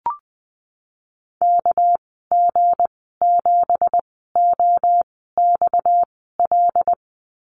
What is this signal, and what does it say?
Morse code, heard as a keyed tone, about 19 wpm, spaced as for 14 wpm, 700 Hz. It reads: KG7OXL
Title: 231 Lesson 37g - Letter x - Call Signs 20wpm